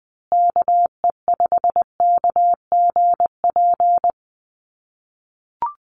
Morse code, heard as a keyed tone, about 20 wpm, 700 Hz. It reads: KE5KGP
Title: USA Call Signs Repeated 3x 20wpm